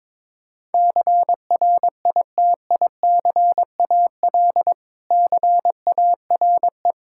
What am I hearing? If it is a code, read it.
Morse code, heard as a keyed tone, about 21 wpm, 700 Hz. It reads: CRITICAL CARE
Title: Practice QSO - Exchange 3 - More Info - 22wpm